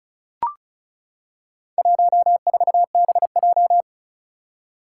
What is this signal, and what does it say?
Morse code, heard as a keyed tone, about 36 wpm, 690 Hz. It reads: 14BJ
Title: Four Challenging Letter-Number Combinations 35wpm